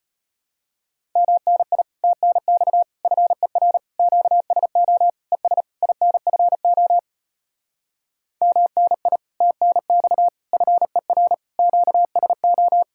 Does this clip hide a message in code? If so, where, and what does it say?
Yes: Morse code from 1.2 to 12.8 s, heard as a keyed tone, about 33 wpm, 700 Hz. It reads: MNI TNX FER QSO ES INFO MNI TNX FER QSO